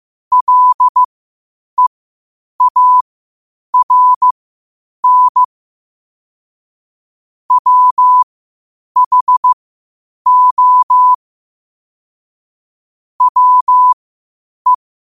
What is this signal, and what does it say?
Morse code, heard as a keyed tone, about 15 wpm, spaced as for 8 wpm, 1000 Hz. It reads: LEARN WHO WE
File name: qotdfiles / 8wpm